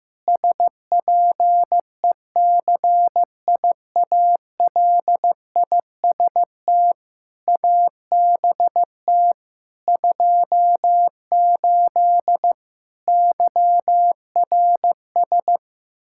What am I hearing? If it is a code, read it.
Morse code, heard as a keyed tone, about 15 wpm, 700 Hz. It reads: SPECIALIST ABT 28 YRS
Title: Practice QSO - Exchange 3 - More Info - 15wpm